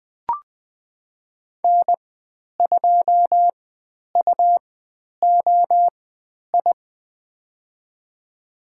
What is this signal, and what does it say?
Morse code, heard as a keyed tone, about 20 wpm, spaced as for 10 wpm, 700 Hz. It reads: N2UOI
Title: 102 Lesson 17g - Letter u - Call Signs 20wpm